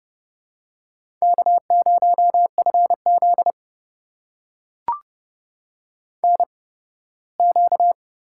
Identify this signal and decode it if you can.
Morse code, heard as a keyed tone, about 30 wpm, spaced as for 18 wpm, 700 Hz. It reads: K0FZ NQ
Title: USA Call Signs - 4 Characters - 8x Character Spacing - 30wpm